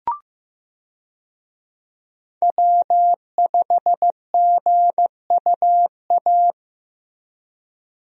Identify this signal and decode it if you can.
Morse code, heard as a keyed tone, about 15 wpm, 700 Hz. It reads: W5GUA
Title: USA Call Signs - 5 Characters - Repeated 2x - 15wpm